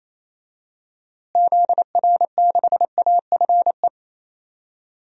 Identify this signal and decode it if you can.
Morse code, heard as a keyed tone, about 28 wpm, 700 Hz. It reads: ZR6AFE
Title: DXCC-100 Call Signs - 6 Characters - Repeated 2x - 28wpm